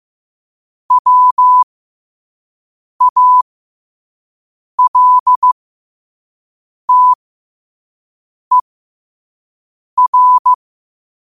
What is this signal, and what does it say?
Morse code, heard as a keyed tone, about 15 wpm, spaced as for 5 wpm, 1000 Hz. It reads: WALTER
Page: qotd-at-5-wpm